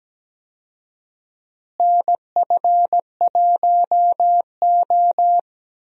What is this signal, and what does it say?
Morse code, heard as a keyed tone, about 17 wpm, 700 Hz. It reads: NF1O
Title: USA Call Signs - 4 Characters - Repeated 2x - 17wpm